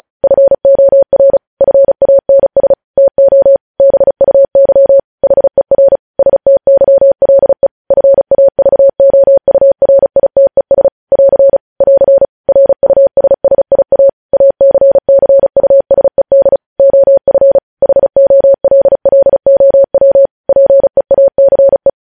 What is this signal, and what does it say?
Morse code, heard as a keyed tone, about 36 wpm, 550 Hz. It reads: FOR FANS TO BUY HER STYLE FAVOURITES <AR> <AR> RUSSIA ACCUSED OF HOLLOW PEACE
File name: podcast / News headlines-1x-35x35